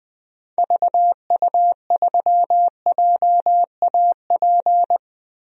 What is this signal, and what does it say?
Morse code, heard as a keyed tone, about 20 wpm, 700 Hz. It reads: VU3JAP
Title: DXCC-100 Call Signs - 6 Characters - Repeated 3x - 20wpm